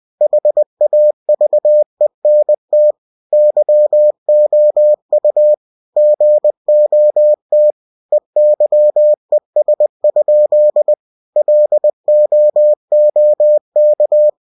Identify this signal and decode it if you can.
Morse code, heard as a keyed tone, about 20 wpm, 600 Hz. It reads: HAVENT YOU GOT EYES? LOOK
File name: cw-books / winnie-the-pooh / winnie-20wpm-page-0072